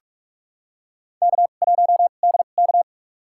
Morse code, heard as a keyed tone, about 44 wpm, spaced as for 32 wpm, 700 Hz. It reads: K1DK
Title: USA Call Signs - 4 Characters - 2x Character Spacing - 45wpm